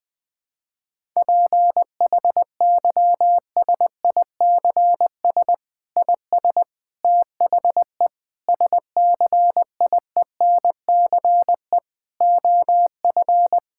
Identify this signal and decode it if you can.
Morse code, heard as a keyed tone, about 20 wpm, 700 Hz. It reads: PHYSICS IS THE SCIENCE OF
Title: Sentences from Top 1000 Words - Review Part 5 - 20wpm